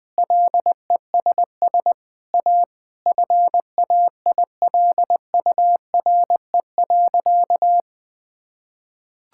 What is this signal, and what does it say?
Morse code, heard as a keyed tone, about 21 wpm, 700 Hz. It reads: LESS A FAILURE.
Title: Sentences from Top 1000 Words - Review Part 6 - 20wpm